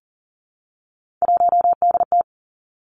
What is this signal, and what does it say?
Morse code, heard as a keyed tone, about 41 wpm, 700 Hz. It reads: 1DT
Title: Three Challenging Letter-Number Combinations 40wpm